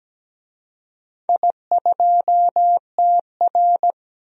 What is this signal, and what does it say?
Morse code, heard as a keyed tone, about 17 wpm, 700 Hz. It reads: I2TR
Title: DXCC-100 Call Signs - 4 Characters - 17wpm